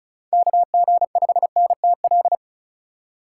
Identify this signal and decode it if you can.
Morse code, heard as a keyed tone, about 36 wpm, 700 Hz. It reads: KG5NTL
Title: USA Call Signs - 6 Characters - 3x Character Spacing - 35wpm